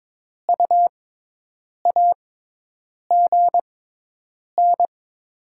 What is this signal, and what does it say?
Morse code, heard as a keyed tone, about 22 wpm, spaced as for 8 wpm, 690 Hz. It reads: U AGN
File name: Top 20 QSO Elements - 6x Character Spacing - 22wpm